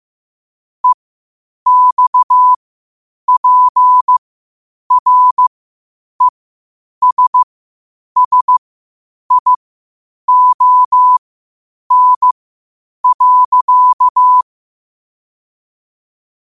Morse code, heard as a keyed tone, about 15 wpm, spaced as for 8 wpm, 1000 Hz. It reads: EXPRESSION.
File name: qotdfiles / 8wpm